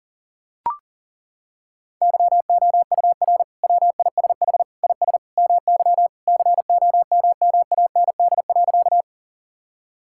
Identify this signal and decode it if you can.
Morse code, heard as a keyed tone, about 40 wpm, 700 Hz. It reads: YOUR WISH IS MY COMMAND.
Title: Sentences from Top 1600 Words 40wpm